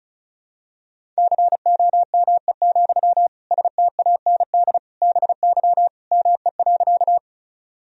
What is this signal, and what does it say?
Morse code, heard as a keyed tone, about 36 wpm, 700 Hz. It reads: COME, STAND BY ME.
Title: Sentences from Top 1000 Words - Review Part 1 - 35wpm